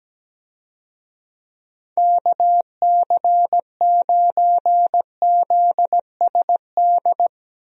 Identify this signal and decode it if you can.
Morse code, heard as a keyed tone, about 17 wpm, 700 Hz. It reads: KC9ZSD
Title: USA Call Signs - 6 Characters - Repeated 3x - 17wpm